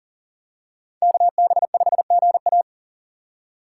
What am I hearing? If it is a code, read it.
Morse code, heard as a keyed tone, about 40 wpm, 700 Hz. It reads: KB5GA
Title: K1USN SST QSO Practice - 40wpm